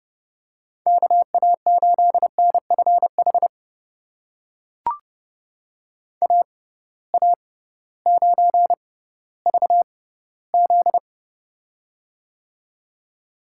Morse code, heard as a keyed tone, about 30 wpm, spaced as for 16 wpm, 700 Hz. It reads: KA8NFH AA9VZ
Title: USA Call Signs - 6x Character Spacing - 30wpm